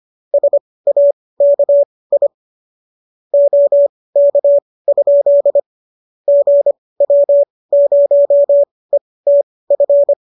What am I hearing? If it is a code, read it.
Morse code, heard as a keyed tone, about 25 wpm, spaced as for 18 wpm, 570 Hz. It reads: SAKI OK? GW0ETF
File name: qso218 18